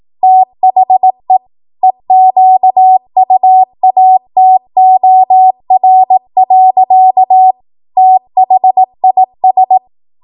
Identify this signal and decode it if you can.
Morse code, heard as a keyed tone, about 18 wpm, 750 Hz. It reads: THE EQUATOR. THIS
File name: Morse / 18WPM